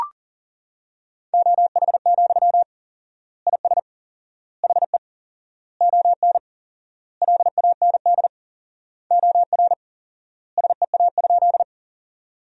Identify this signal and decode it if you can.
Morse code, heard as a keyed tone, about 40 wpm, 700 Hz. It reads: OH, IS HE ON LAND OR SEA?
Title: 4x Word-Spacing - Sentences from Top 500 Words - 40wpm